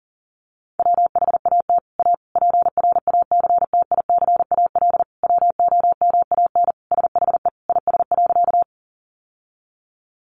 Morse code, heard as a keyed tone, about 40 wpm, 700 Hz. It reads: WHAT A PRACTICAL WOMAN SHE IS.
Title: Sentences from Top 1400 Words 40wpm